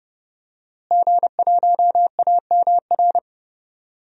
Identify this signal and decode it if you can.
Morse code, heard as a keyed tone, about 30 wpm, 700 Hz. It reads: G1AMR
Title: DXCC-100 Call Signs - 5 Characters - Repeated 3x - 30wpm